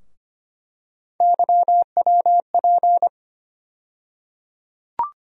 Morse code, heard as a keyed tone, about 25 wpm, 700 Hz. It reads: YWP
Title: USA Call Sign Postfixes - 3 Letters 25wpm